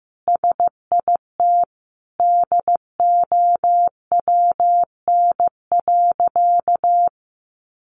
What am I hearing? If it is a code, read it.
Morse code, heard as a keyed tone, about 15 wpm, 700 Hz. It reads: SIT DOWN.
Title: Sentences from Top 1100 Words 15wpm